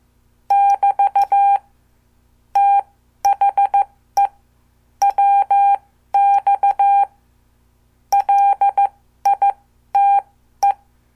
Morse code, heard as a keyed tone, about 14 wpm, spaced as for 12 wpm, 800 Hz. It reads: <BT> THE WX LITE
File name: morse / qso